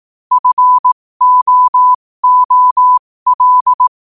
Morse code, heard as a keyed tone, about 18 wpm, spaced as for 15 wpm, 1000 Hz. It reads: FOOL
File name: qotdfiles / 15wpm